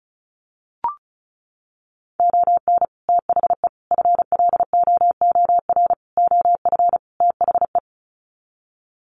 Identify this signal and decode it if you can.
Morse code, heard as a keyed tone, about 34 wpm, 700 Hz. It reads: ON THE FLOOR OF THE
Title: Sets of 5 Words 35wpm